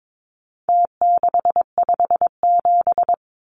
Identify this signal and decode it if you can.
Morse code, heard as a keyed tone, about 22 wpm, 700 Hz. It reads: T657
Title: Numbers - Contest Style - 4-Digit 22wpm